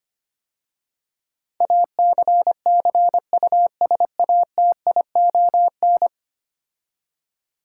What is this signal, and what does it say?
Morse code, heard as a keyed tone, about 25 wpm, 700 Hz. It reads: ACCUSATION
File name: 10 Letter Words - First 3 Letters Given - 25wpm